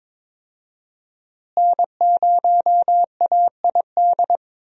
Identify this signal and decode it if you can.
Morse code, heard as a keyed tone, about 22 wpm, 700 Hz. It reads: N0AID
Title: USA Call Signs - 5 Characters - Repeated 2x - 22wpm